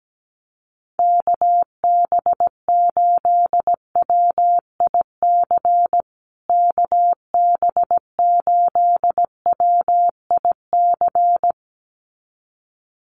Morse code, heard as a keyed tone, about 17 wpm, 700 Hz. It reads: KB8WIC KB8WIC
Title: USA Call Signs Repeated 2x 17wpm